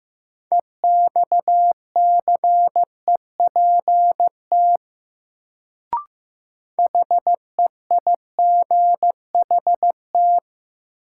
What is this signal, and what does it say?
Morse code, heard as a keyed tone, about 15 wpm, 700 Hz. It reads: EXCEPT HEIGHT
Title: Sentences from Top 1600 Words - New Word Drill - 15wpm